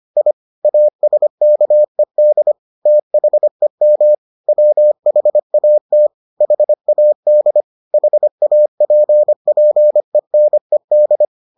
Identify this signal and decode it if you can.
Morse code, heard as a keyed tone, about 25 wpm, 600 Hz. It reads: I ASKED THEM WHAT HAD HAPPENED